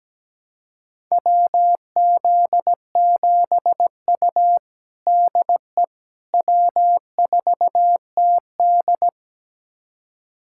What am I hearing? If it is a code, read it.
Morse code, heard as a keyed tone, about 17 wpm, 700 Hz. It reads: WZ7U DE W4TD
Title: USA Callsign DE Callsign - 4 Characters - 17wpm